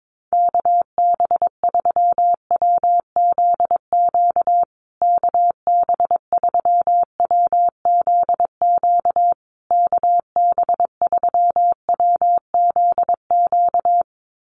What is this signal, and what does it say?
Morse code, heard as a keyed tone, about 22 wpm, 700 Hz. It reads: KB3WZQ KB3WZQ KB3WZQ
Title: USA Call Signs Repeated 3x 22wpm